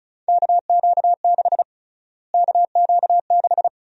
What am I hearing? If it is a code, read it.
Morse code, heard as a keyed tone, about 36 wpm, 700 Hz. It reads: KQ6 KQ6
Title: USA Call Sign Prefixes - Repeated 2x - 35wpm